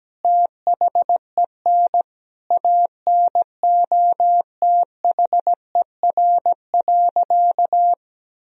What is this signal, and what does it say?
Morse code, heard as a keyed tone, about 17 wpm, 700 Hz. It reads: THEN ANOTHER.
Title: Sentences from Top 1000 Words - Review Part 2 - 17wpm